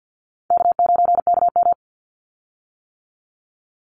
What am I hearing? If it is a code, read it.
Morse code, heard as a keyed tone, about 50 wpm, 700 Hz. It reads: K9KM
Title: Call Signs - 50wpm